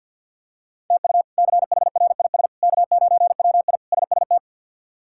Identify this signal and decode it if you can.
Morse code, heard as a keyed tone, about 45 wpm, 700 Hz. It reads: TU CHRIS K9WI SST
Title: K1USN SST QSO Practice - 50wpm